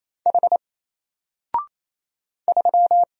Morse code, heard as a keyed tone, about 28 wpm, 700 Hz. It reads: H 3
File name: Single-Letter-Number-28wpm